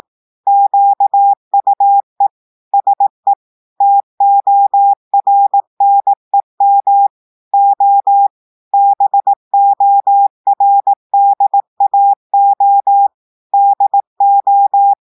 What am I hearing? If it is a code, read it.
Morse code, heard as a keyed tone, about 18 wpm, 800 Hz. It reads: QUE SE TORNEM O BORDAO DO
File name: audios / portugais / portugais texte en clair 2 vitesse 18 mots minutes